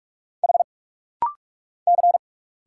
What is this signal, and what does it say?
Morse code, heard as a keyed tone, about 46 wpm, 700 Hz. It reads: H C
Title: Single Letter-Number Alt - Rapid-Fire 45wpm